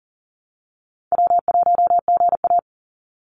Morse code, heard as a keyed tone, about 38 wpm, 700 Hz. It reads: W1GA
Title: USA Call Signs 40wpm